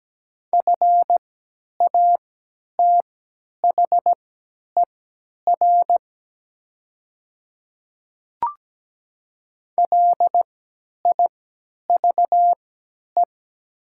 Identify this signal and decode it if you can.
Morse code, heard as a keyed tone, about 17 wpm, spaced as for 10 wpm, 700 Hz. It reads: FATHER LIVE
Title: 3x Character Spacing - Top 200 Words - 17wpm